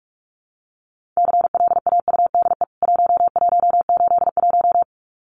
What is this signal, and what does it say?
Morse code, heard as a keyed tone, about 44 wpm, 700 Hz. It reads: CLAUDE 1181
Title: CWT Exchange 45wpm